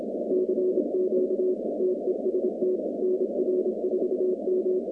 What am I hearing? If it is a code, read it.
Morse code, heard as a keyed tone, about 26 wpm, 370 Hz, with noise. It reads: CQ TEST NR4M
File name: contests / 2019-cw-wpx-kt5j-k5pi- -2330 0-47